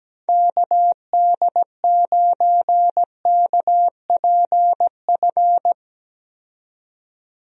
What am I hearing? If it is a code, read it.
Morse code, heard as a keyed tone, about 17 wpm, 700 Hz. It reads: KD9KPF
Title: USA Call Signs - 6 Characters - 17wpm